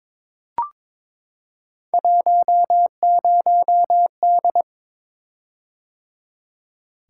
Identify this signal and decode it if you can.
Morse code, heard as a keyed tone, about 21 wpm, 700 Hz. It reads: 10D
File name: practice / 22wpm / Three Letter-Number Combinations - 22wpm